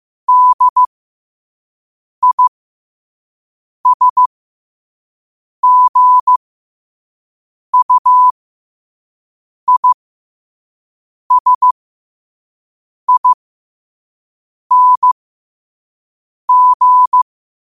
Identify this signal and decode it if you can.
Morse code, heard as a keyed tone, about 15 wpm, spaced as for 5 wpm, 1000 Hz. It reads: DISGUISING
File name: qotdfiles / 5wpm